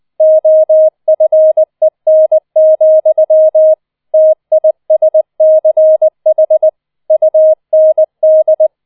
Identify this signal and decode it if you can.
Morse code, heard as a keyed tone, about 20 wpm, 630 Hz. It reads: OFEN, TISCH UND